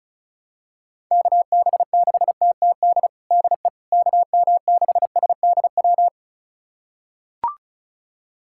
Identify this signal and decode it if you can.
Morse code, heard as a keyed tone, about 36 wpm, 700 Hz. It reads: KB6TTD DE KM6SDW
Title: USA Callsign DE Callsign - 6 Characters - 35wpm